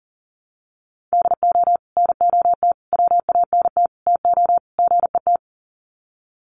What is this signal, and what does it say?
Morse code, heard as a keyed tone, about 37 wpm, 700 Hz. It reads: DO NOT WANT TO GET
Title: Sets of 5 Words 40wpm